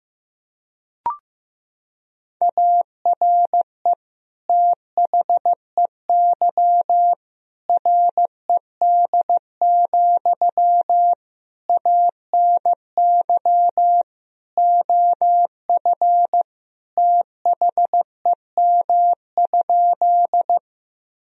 Morse code, heard as a keyed tone, about 15 wpm, 700 Hz. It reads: ARE THEY RED, ANY OF THEM?
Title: Sentences from Top 1000 Words - Review Part 2 - 15wpm